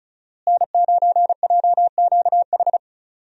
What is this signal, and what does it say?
Morse code, heard as a keyed tone, about 36 wpm, 700 Hz. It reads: N9JQH
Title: USA Call Signs - 5 Characters - Repeated 2x - 35wpm